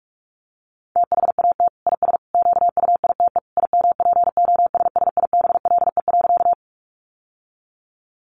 Morse code, heard as a keyed tone, about 44 wpm, 700 Hz. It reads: THAT IS QUITE IMPOSSIBLE.
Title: Sentences from Top 1100 Words 45wpm